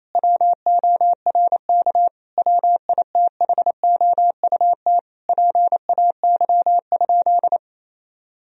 Morse code, heard as a keyed tone, about 28 wpm, 700 Hz. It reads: WORK WITHOUT PAY?
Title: Sentences from Top 1000 Words - Review Part 3 - 28wpm